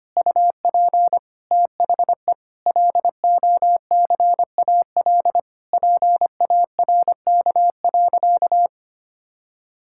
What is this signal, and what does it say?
Morse code, heard as a keyed tone, about 25 wpm, 700 Hz. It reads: UP THE LOCAL PARK.
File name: practice / 25wpm / Difficult Sentences with Hints - 25wpm